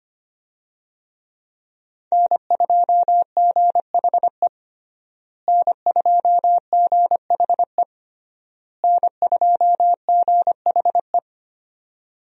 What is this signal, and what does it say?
Morse code, heard as a keyed tone, about 25 wpm, 700 Hz. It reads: N2GHE N2GHE N2GHE
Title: USA Call Signs - 5 Characters - Repeated 3x - 25wpm